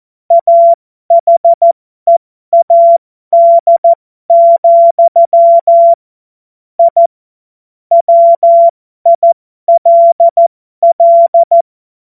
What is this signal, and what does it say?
Morse code, heard as a keyed tone, about 14 wpm, spaced as for 12 wpm, 680 Hz. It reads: AHEAD, I WILL
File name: Morsekurs / Witze